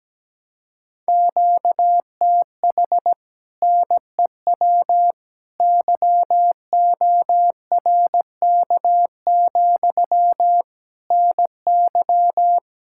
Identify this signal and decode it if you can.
Morse code, heard as a keyed tone, about 17 wpm, 700 Hz. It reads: QTH NEW YORK, NY
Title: Speed-Racing Common QSO Phrases Encore - 17wpm